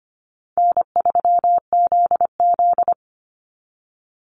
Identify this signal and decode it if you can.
Morse code, heard as a keyed tone, about 25 wpm, 700 Hz. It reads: N3ZZ
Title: Easy Call Signs Repeated 3x - 25wpm